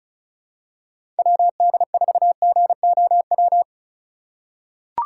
Morse code, heard as a keyed tone, about 36 wpm, 700 Hz. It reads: WD4GOW
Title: USA Call Signs - 6 Characters - 3x Character Spacing - 35wpm